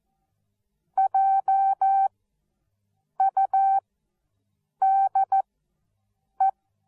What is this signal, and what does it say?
Morse code, heard as a keyed tone, about 14 wpm, spaced as for 6 wpm, 780 Hz. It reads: JUDE
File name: Onlinelehrgang / morsen / DJ4UF Morsen13